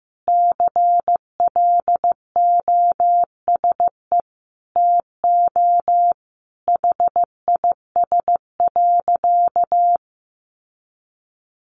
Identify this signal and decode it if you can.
Morse code, heard as a keyed tone, about 15 wpm, 700 Hz. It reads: CLOSE TO HIS.
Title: Sentences from Top 900 Words 15wpm